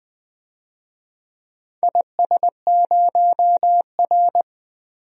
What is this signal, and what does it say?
Morse code, heard as a keyed tone, about 20 wpm, 700 Hz. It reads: IS0R
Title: DXCC-100 Call Signs - 4 Characters - 20wpm